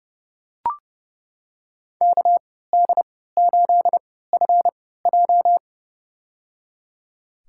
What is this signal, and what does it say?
Morse code, heard as a keyed tone, about 30 wpm, spaced as for 17 wpm, 700 Hz. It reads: KD8FJ
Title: USA Call Signs - 5 Characters - 3x Character Spacing - 30wpm